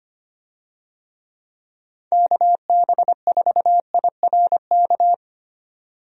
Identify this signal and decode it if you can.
Morse code, heard as a keyed tone, about 25 wpm, 700 Hz. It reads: KB4IRK
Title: USA Call Signs - 6 Characters - 25wpm